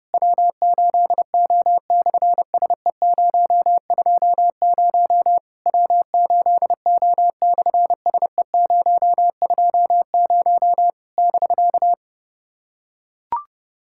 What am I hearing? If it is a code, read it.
Morse code, heard as a keyed tone, about 28 wpm, 700 Hz. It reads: W8O/SE020 W8O/SE020 <BK>
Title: SOTA QSO Practice 30wpm